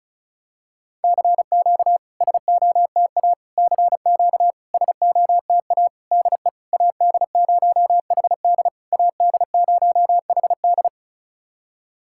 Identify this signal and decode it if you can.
Morse code, heard as a keyed tone, about 34 wpm, 690 Hz. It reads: CQ SOTA CQ SOTA DE AD0HD AD0HD
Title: SOTA QSO Practice 35wpm